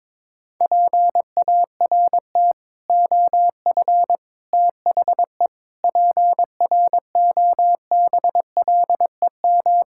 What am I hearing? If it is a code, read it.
Morse code, heard as a keyed tone, about 22 wpm, 700 Hz. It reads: PART OF THE PROBLEM
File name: practice / 22wpm / Sets of 6 Words - 22wpm